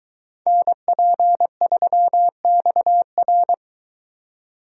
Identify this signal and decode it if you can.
Morse code, heard as a keyed tone, about 22 wpm, 700 Hz. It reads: NP3XR